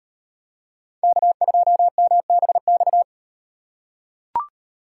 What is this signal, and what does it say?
Morse code, heard as a keyed tone, about 36 wpm, 700 Hz. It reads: K2MBX